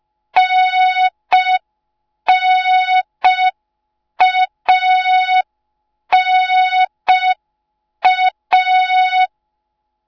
Morse code, heard as a keyed tone, about 5 wpm, 3700 Hz. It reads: NNANA